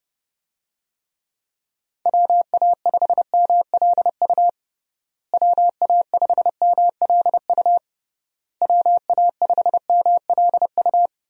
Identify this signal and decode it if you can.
Morse code, heard as a keyed tone, about 30 wpm, 700 Hz. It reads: WA5MLU WA5MLU WA5MLU
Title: USA Call Signs - 6 Characters - Repeated 3x - 30wpm